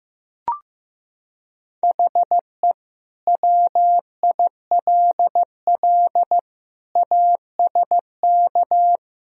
Morse code, heard as a keyed tone, about 15 wpm, 700 Hz. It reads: HE WILL ASK
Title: Sentences from Top 200 Words and Abbreviations 15wpm